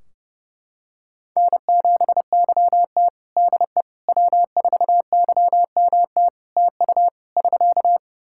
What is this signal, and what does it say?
Morse code, heard as a keyed tone, about 30 wpm, 700 Hz. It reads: N7YT DE W4YMT TU <SK>
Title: K1USN SST QSO Practice 30wpm